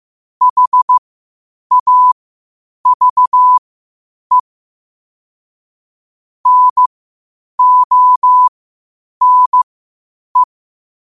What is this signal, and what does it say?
Morse code, heard as a keyed tone, about 15 wpm, spaced as for 8 wpm, 1000 Hz. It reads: HAVE NONE
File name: qotdfiles / 8wpm